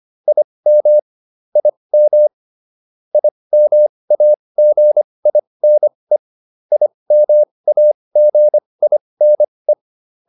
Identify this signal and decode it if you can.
Morse code, heard as a keyed tone, about 25 wpm, spaced as for 20 wpm, 600 Hz. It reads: IM IM IMAGINE IMAGINE